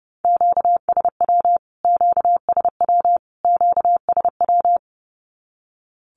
Morse code, heard as a keyed tone, about 30 wpm, 700 Hz. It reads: QSW QSW QSW
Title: Q-Codes Repeated 3x 30wpm